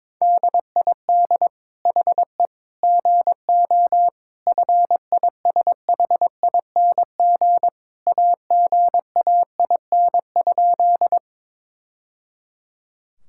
Morse code, heard as a keyed tone, about 22 wpm, 700 Hz. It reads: DID HE GO FISHING AGAIN?
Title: Sentences from Top 1000 Words - Review Part 5 - 22wpm